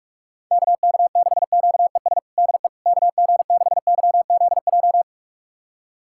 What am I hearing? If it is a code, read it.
Morse code, heard as a keyed tone, about 46 wpm, 700 Hz. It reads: KK6QES DE KG6YZJ
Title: USA Callsign DE Callsign - 6 Characters - 45wpm